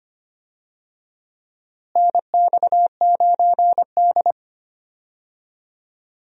Morse code, heard as a keyed tone, about 25 wpm, 700 Hz. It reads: NX9D
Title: USA Call Signs - 4 Characters - Repeated 2x - 25wpm